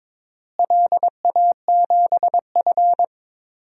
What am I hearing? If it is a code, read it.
Morse code, heard as a keyed tone, about 21 wpm, 700 Hz. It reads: LA7F
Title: DXCC-100 Call Signs - 4 Characters - Repeated 2x - 22wpm